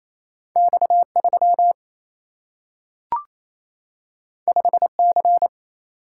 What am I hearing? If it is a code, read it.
Morse code, heard as a keyed tone, about 28 wpm, 700 Hz. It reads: X3 5C
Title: Single-Letter Single-Number 28wpm